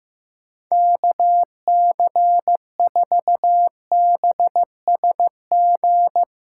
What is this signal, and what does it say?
Morse code, heard as a keyed tone, about 15 wpm, 700 Hz. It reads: KC4BSG
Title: USA Call Signs - 8x Character Spacing - 15wpm